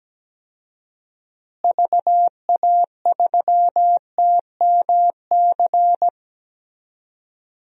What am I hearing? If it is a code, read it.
Morse code, heard as a keyed tone, about 17 wpm, 700 Hz. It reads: VA3TMC
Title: DXCC-100 Call Signs - 6 Characters - Repeated 2x - 17wpm